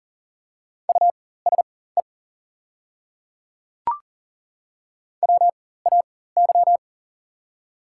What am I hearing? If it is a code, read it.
Morse code, heard as a keyed tone, about 39 wpm, spaced as for 19 wpm, 700 Hz. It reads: USE WAY
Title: 4x Character Spacing - Top 200 Words - 40wpm